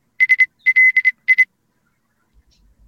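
Morse code, heard as a keyed tone, about 26 wpm, spaced as for 21 wpm, 2050 Hz. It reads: SLI
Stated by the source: Live amateur radio HF contacts attempted on the air.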